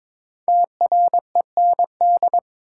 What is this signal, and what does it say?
Morse code, heard as a keyed tone, about 22 wpm, 700 Hz. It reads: TREND
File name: 5 Letter Words - First 3 Letters Given - 22wpm